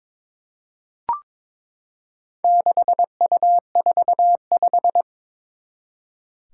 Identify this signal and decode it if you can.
Morse code, heard as a keyed tone, about 22 wpm, 700 Hz. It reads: 6U45
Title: Four Challenging Letter-Number Combinations 22wpm